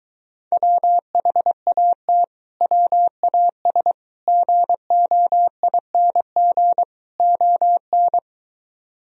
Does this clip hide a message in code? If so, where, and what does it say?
Yes: Morse code from 0.5 to 8.2 s, heard as a keyed tone, about 23 wpm, 700 Hz. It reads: WHAT WAS GOING ON